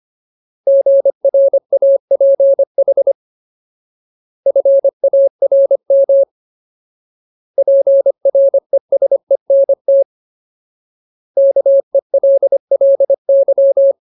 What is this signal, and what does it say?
Morse code, heard as a keyed tone, about 25 wpm, 550 Hz. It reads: GRAPH FARM PRESENT KELLY
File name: Ultimate Word List / 25WPM Ultimate